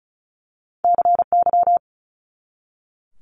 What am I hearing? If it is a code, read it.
Morse code, heard as a keyed tone, about 36 wpm, 700 Hz. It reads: CY
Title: Multi-letter Phonemes 35wpm